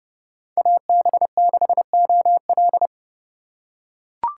Morse code, heard as a keyed tone, about 30 wpm, 700 Hz. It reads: AB6OL